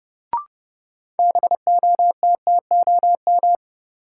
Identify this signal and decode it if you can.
Morse code, heard as a keyed tone, about 30 wpm, 700 Hz. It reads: BOTTOM
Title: Sets of 3 Words - New Word Drill - 30wpm